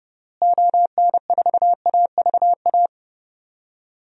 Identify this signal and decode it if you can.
Morse code, heard as a keyed tone, about 32 wpm, 700 Hz. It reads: ON4AVA
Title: DXCC-100 Call Signs - 6 Characters - Repeated 3x - 30wpm